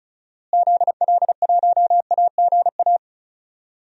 Morse code, heard as a keyed tone, about 34 wpm, 700 Hz. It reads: ZL1AGA
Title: DXCC-100 Call Signs - 6 Characters - Repeated 3x - 35wpm